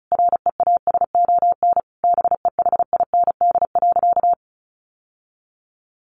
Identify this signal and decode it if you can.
Morse code, heard as a keyed tone, about 34 wpm, 700 Hz. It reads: REASON BEHIND.
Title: Sentences from Top 700 Words 35wpm